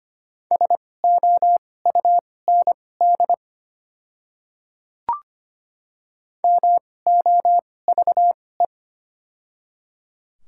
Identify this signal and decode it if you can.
Morse code, heard as a keyed tone, about 25 wpm, spaced as for 18 wpm, 690 Hz. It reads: SOUND MOVE
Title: 2x Character Spacing - Top 300 Words - 25wpm